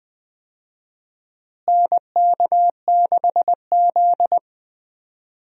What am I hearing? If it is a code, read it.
Morse code, heard as a keyed tone, about 20 wpm, 700 Hz. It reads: NK6Z
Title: USA Call Signs - 4 Characters - Repeated 3x - 20wpm